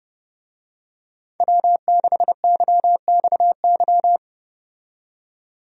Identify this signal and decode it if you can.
Morse code, heard as a keyed tone, about 32 wpm, 700 Hz. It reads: W6YXY
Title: USA Call Signs - 5 Characters - 30wpm